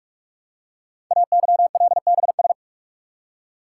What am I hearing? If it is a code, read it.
Morse code, heard as a keyed tone, about 44 wpm, 700 Hz. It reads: AYLBS
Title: Code Groups - 5 Letter:Numbers - 45wpm